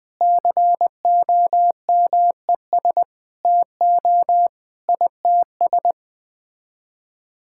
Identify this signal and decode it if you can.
Morse code, heard as a keyed tone, about 20 wpm, 700 Hz. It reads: COMES TO ITS
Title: Sets of 9 Words - 20wpm